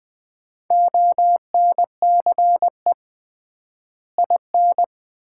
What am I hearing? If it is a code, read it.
Morse code, heard as a keyed tone, about 21 wpm, 700 Hz. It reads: ONCE IN
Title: 3x Word-Spacing - Sets of 4 Words - 20wpm